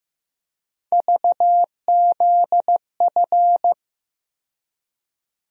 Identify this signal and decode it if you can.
Morse code, heard as a keyed tone, about 15 wpm, 700 Hz. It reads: VZF
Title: USA Call Sign Postfixes - 3 Letters 15wpm